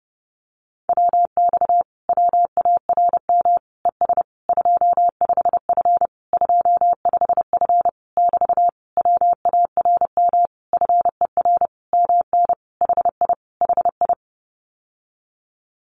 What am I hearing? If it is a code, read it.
Morse code, heard as a keyed tone, about 32 wpm, 700 Hz. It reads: WX WARM ES 25F 25F <BT> WARM FER MN HI HI
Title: Common QSO Phrases Encore 30wpm